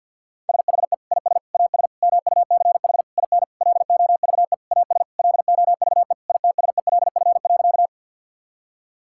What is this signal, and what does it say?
Morse code, heard as a keyed tone, about 50 wpm, 700 Hz. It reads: SHE IS AS MUCH IN LOVE AS LOVE ITSELF.
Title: Sentences from Top 1000 Words - Review Part 5 - 50wpm